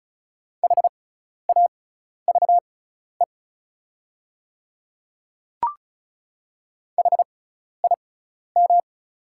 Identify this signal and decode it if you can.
Morse code, heard as a keyed tone, about 35 wpm, spaced as for 12 wpm, 690 Hz. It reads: HAVE HIM
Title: 6x Character Spacing - Top 100 Words - 35wpm